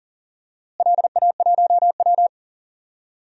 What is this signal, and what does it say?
Morse code, heard as a keyed tone, about 40 wpm, 700 Hz. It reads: LA1W
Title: DXCC-100 Call Signs - 4 Characters - Repeated 3x - 40wpm